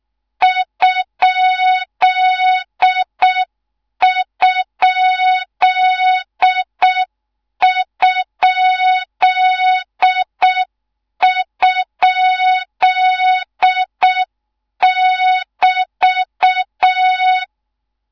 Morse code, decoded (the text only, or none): ????<BT>